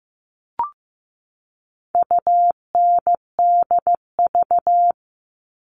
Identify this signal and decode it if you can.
Morse code, heard as a keyed tone, about 15 wpm, 700 Hz. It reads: UNDV
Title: All Cut Numbers - 4-Digit 15wpm